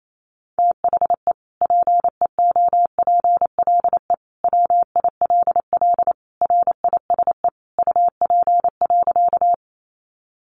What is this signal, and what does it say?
Morse code, heard as a keyed tone, about 28 wpm, 700 Hz. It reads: THE PEOPLE WILL RISE UP.
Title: Sentences from Top 900 Words 28wpm